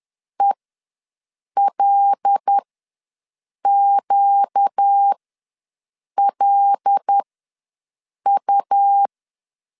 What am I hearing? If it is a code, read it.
Morse code, heard as a keyed tone, about 10 wpm, spaced as for 6 wpm, 800 Hz. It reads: ELQLU
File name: LEZIONI / 8 YJ45 / 8 Y ws 10wpm 3952